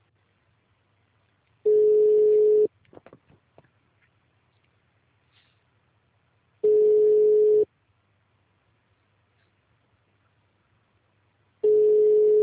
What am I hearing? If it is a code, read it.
Ringback tone (Europe, 425 Hz).
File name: out- -2020- - - .18